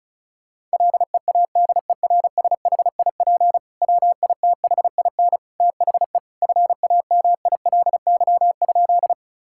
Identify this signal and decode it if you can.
Morse code, heard as a keyed tone, about 36 wpm, 700 Hz. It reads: LEADERSHIP WITHIN THE FAMILY?
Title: Sentences from Top 1600 Words 35wpm